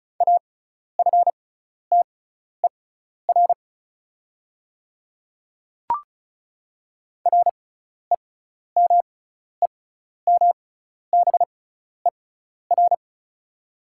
Morse code, heard as a keyed tone, about 36 wpm, spaced as for 12 wpm, 690 Hz. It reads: AFTER REMEMBER
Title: 6x Character Spacing - Top 500 Words - 35wpm